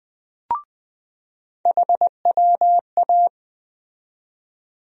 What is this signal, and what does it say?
Morse code, heard as a keyed tone, about 20 wpm, 700 Hz. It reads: HWA
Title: USA Call Sign Postfixes - 3 Letters 20wpm